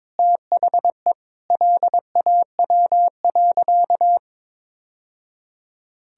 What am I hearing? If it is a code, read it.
Morse code, heard as a keyed tone, about 22 wpm, 700 Hz. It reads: THE LAW.
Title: Sentences from Top 1000 Words - Review Part 3 - 22wpm